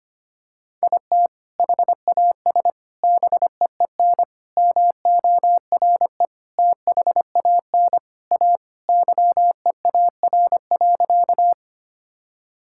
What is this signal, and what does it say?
Morse code, heard as a keyed tone, about 25 wpm, 700 Hz. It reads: IT HAS BEEN MORE THAN A YEAR.
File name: practice / 25wpm / Sentences with 4-Word Phrases - 25wpm